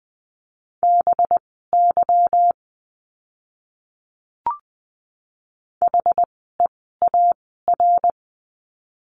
Morse code, heard as a keyed tone, about 20 wpm, spaced as for 14 wpm, 700 Hz. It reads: BY HEAR
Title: 206 Lesson 33f - Letter j - Words 20wpm